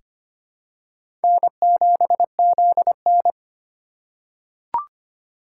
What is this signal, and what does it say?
Morse code, heard as a keyed tone, about 25 wpm, 700 Hz. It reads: N7ZN
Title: K1USN SST QSO Practice - 25wpm